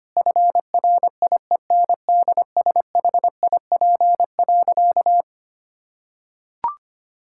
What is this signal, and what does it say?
Morse code, heard as a keyed tone, about 25 wpm, 700 Hz. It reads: FRIENDSHIP.
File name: practice / 25wpm / Difficult Sentences with Hints - 25wpm